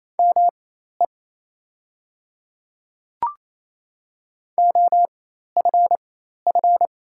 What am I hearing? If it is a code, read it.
Morse code, heard as a keyed tone, about 28 wpm, spaced as for 13 wpm, 700 Hz. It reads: ME OFF
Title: 4x Character Spacing - Top 300 Words - 28wpm